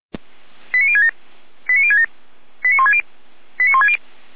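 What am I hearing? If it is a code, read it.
Five-tone selective call (ZVEI), heard as quick runs of five tones, tones of about 70 ms. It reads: 89067 89067 89160 89160
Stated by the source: ZVEI five-tone selective call received with an SDR.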